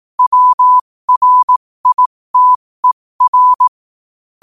Morse code, heard as a keyed tone, about 18 wpm, spaced as for 15 wpm, 1000 Hz. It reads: WRITER
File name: qotdfiles / 15wpm